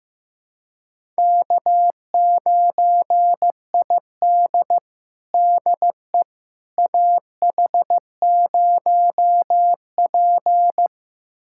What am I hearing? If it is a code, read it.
Morse code, heard as a keyed tone, about 15 wpm, 700 Hz. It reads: K9ID DE AH0P
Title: USA Callsign DE Callsign - 4 Characters - 15wpm